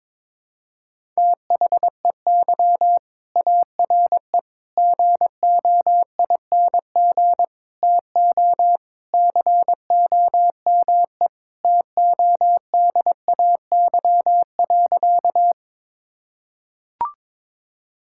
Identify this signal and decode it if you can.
Morse code, heard as a keyed tone, about 22 wpm, 700 Hz. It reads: THEY ARE GOING TO COME TODAY.
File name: practice / 22wpm / Sentences with 3-Word Phrases - 22wpm